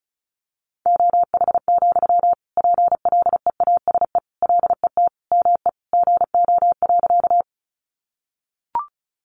Morse code, heard as a keyed tone, about 34 wpm, 700 Hz. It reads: OH, PLEASE LET ME GO.
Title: Sentences from Top 1500 Words 35wpm